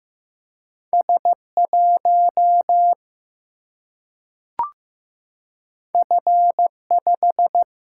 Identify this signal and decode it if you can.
Morse code, heard as a keyed tone, about 15 wpm, 700 Hz. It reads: S 1 F 5
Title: Single-Letter Single-Number 15wpm